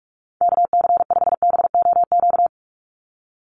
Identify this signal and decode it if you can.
Morse code, heard as a keyed tone, about 44 wpm, 700 Hz. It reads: KC5BOQ
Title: USA Call Signs 45wpm